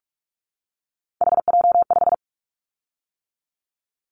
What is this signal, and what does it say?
Morse code, heard as a keyed tone, about 46 wpm, 700 Hz. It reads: HJ5
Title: Callsign Prefixes for Top 100 DXCC - 45wpm